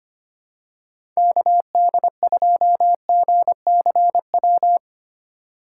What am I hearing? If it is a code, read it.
Morse code, heard as a keyed tone, about 25 wpm, 700 Hz. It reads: KD2GCW
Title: USA Call Signs - 2x Character Spacing - 25wpm